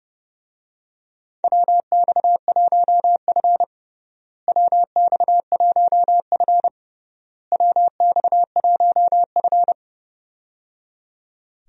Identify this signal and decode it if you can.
Morse code, heard as a keyed tone, about 30 wpm, 700 Hz. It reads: WX1F WX1F WX1F
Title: USA Call Signs - 4 Characters - Repeated 3x - 30wpm